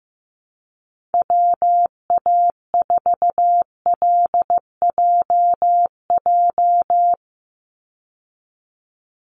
Morse code, heard as a keyed tone, about 15 wpm, 700 Hz. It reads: WA4LJJ
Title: Call Signs Repeated 3x - 15wpm